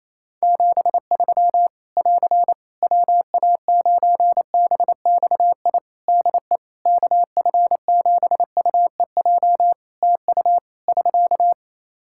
Morse code, heard as a keyed tone, about 28 wpm, 700 Hz. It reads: 73 <AR> WA9BXI DE KF7UEJ TU <SK>
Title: K1USN SST QSO Practice 28wpm